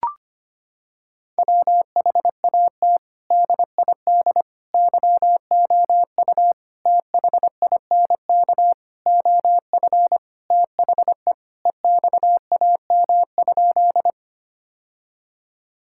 Morse code, heard as a keyed tone, about 25 wpm, 700 Hz. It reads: WHAT DID YOU THINK OF THE EXAM?
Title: Sentences from Top 1000 Words - Review Part 5 - 25wpm